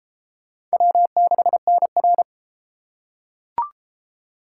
Morse code, heard as a keyed tone, about 33 wpm, 700 Hz. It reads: W6NR